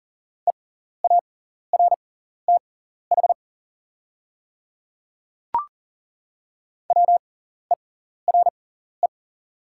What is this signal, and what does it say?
Morse code, heard as a keyed tone, about 38 wpm, spaced as for 14 wpm, 700 Hz. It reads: EARTH WERE
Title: 6x Character Spacing - Top 500 Words - 40wpm